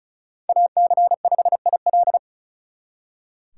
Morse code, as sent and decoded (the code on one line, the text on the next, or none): .- -.-. ..... .. .-..
AC5IL